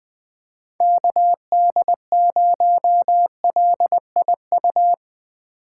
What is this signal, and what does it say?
Morse code, heard as a keyed tone, about 20 wpm, 700 Hz. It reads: KD0LIU
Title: Speed Racing - USA 2x3 Call Signs 17wpm